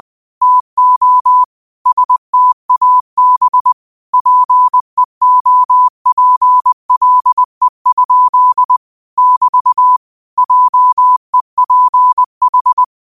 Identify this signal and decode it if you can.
Morse code, heard as a keyed tone, about 20 wpm, 1000 Hz. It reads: TO STAB PEOPLE? <BT> JEPH
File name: qotdfiles / 20wpm